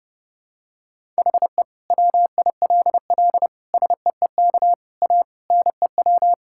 Morse code, heard as a keyed tone, about 32 wpm, 700 Hz. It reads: HE WILL SEEK A NEW
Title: Sentences from Top 1600 Words 30wpm